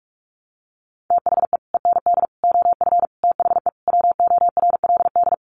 Morse code, heard as a keyed tone, about 44 wpm, 700 Hz. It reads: THE END OF THE WORLD
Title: Sets of 5 Words 45wpm